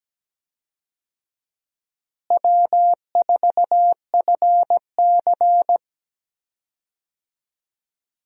Morse code, heard as a keyed tone, about 17 wpm, 700 Hz. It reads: W4FC
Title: USA Call Signs - 4 Characters - Repeated 2x - 17wpm